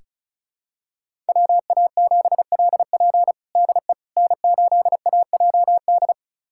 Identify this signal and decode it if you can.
Morse code, heard as a keyed tone, about 36 wpm, 700 Hz. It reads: WA7LP DE N8AJD
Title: USA Callsign DE Callsign - 5 Characters - 35wpm